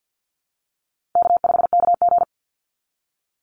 Morse code, heard as a keyed tone, about 50 wpm, 700 Hz. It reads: K5KG
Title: Call Signs - 50wpm